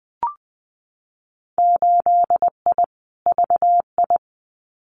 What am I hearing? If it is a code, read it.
Morse code, heard as a keyed tone, about 20 wpm, 700 Hz. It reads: 8I VI
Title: Winter Field Day Exchanges Encore 20wpm